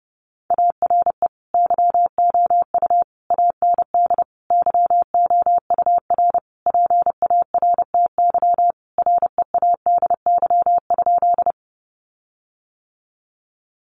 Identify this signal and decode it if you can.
Morse code, heard as a keyed tone, about 30 wpm, 700 Hz. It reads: ARE YOU AND YOUR PARTY READY?
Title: Sentences from Top 1400 Words 30wpm